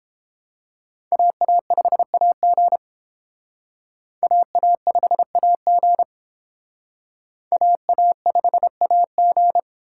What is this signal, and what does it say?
Morse code, heard as a keyed tone, about 29 wpm, 700 Hz. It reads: AA5AG AA5AG AA5AG